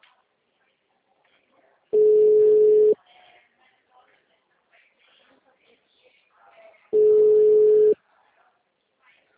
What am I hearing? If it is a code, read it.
Ringback tone (Europe, 425 Hz), with speech.
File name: out- -2019- - - .2949